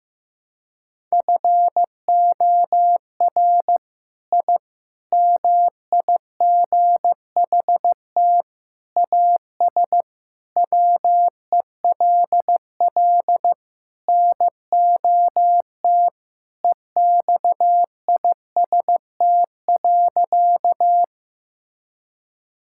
Morse code, heard as a keyed tone, about 15 wpm, 700 Hz. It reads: FOR I MIGHT AS WELL NOT EXIST.
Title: Sentences from Top 1000 Words - Review Part 5 - 15wpm